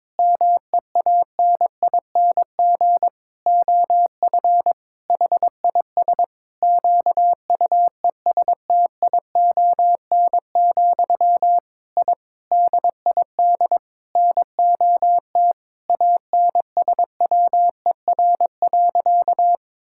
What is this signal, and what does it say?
Morse code, heard as a keyed tone, about 22 wpm, 700 Hz. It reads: MEANING OF HIS QUESTION, I DID NOT ANSWER.
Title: Sentences from Top 1000 Words - Review Part 6 - 22wpm